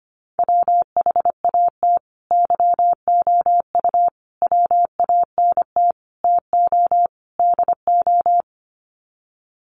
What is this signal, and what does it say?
Morse code, heard as a keyed tone, about 25 wpm, 700 Hz. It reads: WHAT YOU WANT TO DO
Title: Sets of 5 Words Encore 25wpm